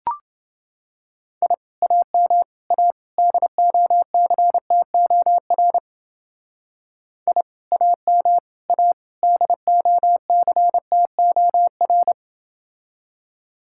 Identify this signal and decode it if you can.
Morse code, heard as a keyed tone, about 28 wpm, 700 Hz. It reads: I AM A DOCTOR I AM A DOCTOR